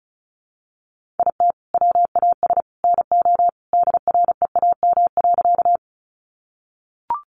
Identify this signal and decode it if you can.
Morse code, heard as a keyed tone, about 34 wpm, 700 Hz. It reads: IT WAS NO DREAM.
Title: Sentences from Top 1500 Words 35wpm